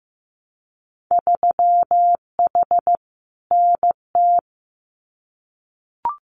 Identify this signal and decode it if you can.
Morse code, heard as a keyed tone, about 15 wpm, 700 Hz. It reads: 3H NT
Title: Winter Field Day Exchanges Encore 15wpm